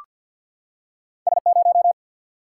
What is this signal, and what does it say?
Morse code, heard as a keyed tone, about 53 wpm, 700 Hz. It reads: S0
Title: Single-Letter Single-Number 50wpm